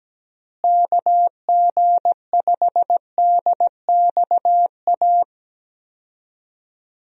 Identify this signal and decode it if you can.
Morse code, heard as a keyed tone, about 17 wpm, 700 Hz. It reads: KG5DXA
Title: USA Call Signs - 6 Characters - Repeated 2x - 17wpm